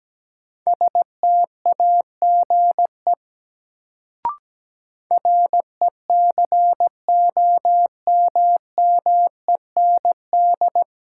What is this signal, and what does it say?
Morse code, heard as a keyed tone, about 17 wpm, 700 Hz. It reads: STAGE RECOMMEND
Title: Sentences from Top 1200 Words - New Word Drill Alt - 17wpm